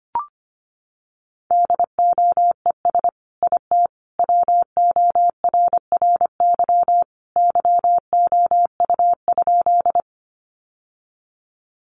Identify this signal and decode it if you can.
Morse code, heard as a keyed tone, about 25 wpm, 700 Hz. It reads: DOES IT WORRY YOU?
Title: Sentences from Top 1100 Words 25wpm